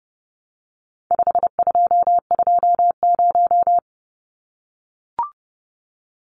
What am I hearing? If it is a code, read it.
Morse code, heard as a keyed tone, about 30 wpm, 700 Hz. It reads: 5220
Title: Numbers - 4-Digit 30wpm